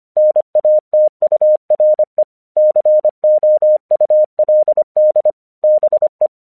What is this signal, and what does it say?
Morse code, heard as a keyed tone, about 25 wpm, 610 Hz. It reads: NATURE COULD BE
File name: 1 English 25wpm / AQC05